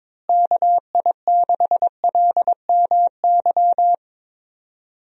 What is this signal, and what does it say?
Morse code, heard as a keyed tone, about 22 wpm, 700 Hz. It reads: KI6LMY
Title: USA Call Signs - 6 Characters - Repeated 3x - 22wpm